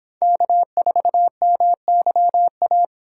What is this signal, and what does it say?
Morse code, heard as a keyed tone, about 26 wpm, 700 Hz. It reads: K4MYA